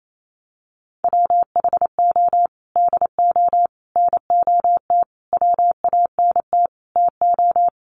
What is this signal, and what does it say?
Morse code, heard as a keyed tone, about 28 wpm, 700 Hz. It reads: WHO DO NOT WANT TO
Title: Sets of 5 Words Encore2 28wpm